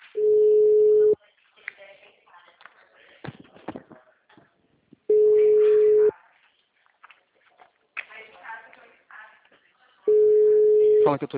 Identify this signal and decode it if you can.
Ringback tone (Europe, 425 Hz), with speech.